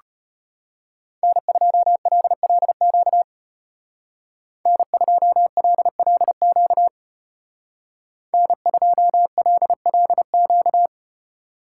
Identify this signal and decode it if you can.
Morse code, heard as a keyed tone, about 32 wpm, 700 Hz. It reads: N2LLQ N2LLQ N2LLQ